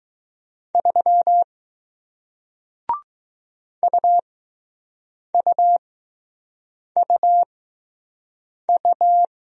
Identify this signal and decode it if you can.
Morse code, heard as a keyed tone, about 20 wpm, spaced as for 6 wpm, 700 Hz. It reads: 3 UUUU